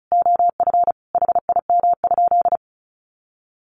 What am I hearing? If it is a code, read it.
Morse code, heard as a keyed tone, about 34 wpm, 700 Hz. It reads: OF HIM?
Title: Sentences from Top 600 Words 35wpm